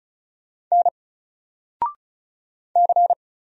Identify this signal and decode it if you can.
Morse code, heard as a keyed tone, about 36 wpm, 700 Hz. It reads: N C
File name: Single-Letter-Number-35wpm